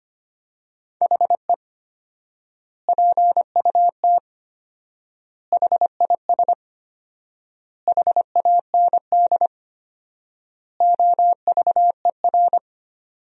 Morse code, heard as a keyed tone, about 25 wpm, 700 Hz. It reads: HE PUT HIS HAND OVER
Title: 4x Word-Spacing - Sentences from Top 400 Words - 25wpm